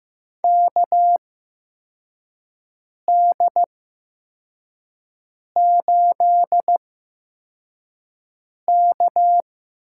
Morse code, heard as a keyed tone, about 15 wpm, spaced as for 4 wpm, 700 Hz. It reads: KD8K
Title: USA Call Signs - 8x Character Spacing - 15wpm